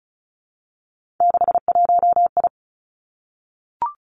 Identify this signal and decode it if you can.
Morse code, heard as a keyed tone, about 36 wpm, 700 Hz. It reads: 61I
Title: Three Challenging Letter-Number Combinations 35wpm